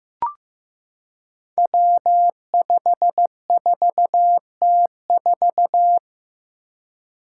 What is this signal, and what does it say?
Morse code, heard as a keyed tone, about 15 wpm, 700 Hz. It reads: W54T4
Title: Five Challenging Letter-Number Combinations 15wpm